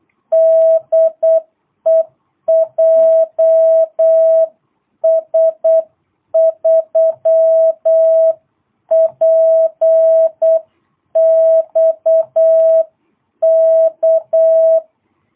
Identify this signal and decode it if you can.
Morse code, heard as a keyed tone, about 8 wpm, 660 Hz. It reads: DE JS3PXK
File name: jh8cbh / 2022cw